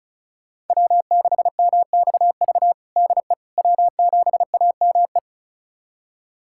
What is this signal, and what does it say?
Morse code, heard as a keyed tone, about 36 wpm, 700 Hz. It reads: W6MXV DE W7AME
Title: USA Callsign DE Callsign - 5 Characters - 35wpm